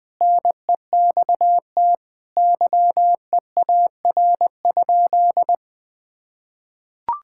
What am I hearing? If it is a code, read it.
Morse code, heard as a keyed tone, about 20 wpm, 700 Hz. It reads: NEXT YEAR?
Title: Sentences from Top 1000 Words - Review Part 5 - 20wpm